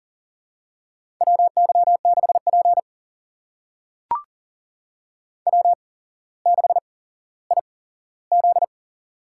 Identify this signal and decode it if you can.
Morse code, heard as a keyed tone, about 40 wpm, spaced as for 17 wpm, 700 Hz. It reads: WY6P W6IZ
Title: USA Call Signs - 4 Characters - 8x Character Spacing - 40wpm